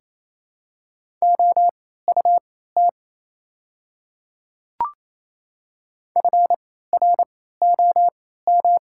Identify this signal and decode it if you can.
Morse code, heard as a keyed tone, about 28 wpm, spaced as for 16 wpm, 690 Hz. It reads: OUT FROM
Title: 3x Character Spacing - Top 100 Words - 28wpm